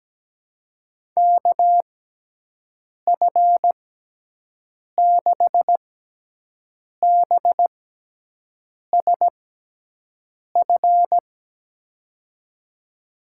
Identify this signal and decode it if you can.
Morse code, heard as a keyed tone, about 17 wpm, spaced as for 6 wpm, 700 Hz. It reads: KF6BSF
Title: USA Call Signs - 6 Characters - 6x Character Spacing - 17wpm